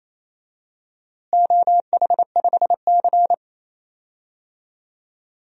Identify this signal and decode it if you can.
Morse code, heard as a keyed tone, about 28 wpm, 700 Hz. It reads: OH5C
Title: DXCC-100 Call Signs - 4 Characters - Repeated 3x - 28wpm